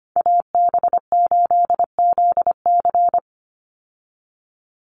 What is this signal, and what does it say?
Morse code, heard as a keyed tone, about 25 wpm, 700 Hz. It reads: AB8ZC
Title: USA Call Signs 25wpm